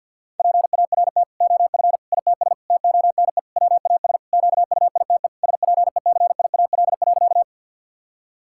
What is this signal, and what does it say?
Morse code, heard as a keyed tone, about 50 wpm, 700 Hz. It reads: PART OF ITS TONE WAS QUITE SPECIAL.
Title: Sentences from Top 1600 Words 50wpm